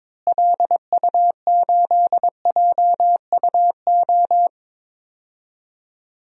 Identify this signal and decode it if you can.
Morse code, heard as a keyed tone, about 21 wpm, 700 Hz. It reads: LU8JUO
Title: DXCC-100 Call Signs - 6 Characters - Repeated 3x - 22wpm